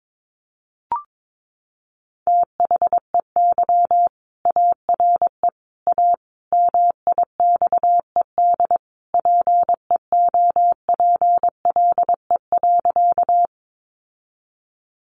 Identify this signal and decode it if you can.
Morse code, heard as a keyed tone, about 22 wpm, 700 Hz. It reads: THEY ARE A MIXED PEOPLE.
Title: Sentences from Top 1500 Words 22wpm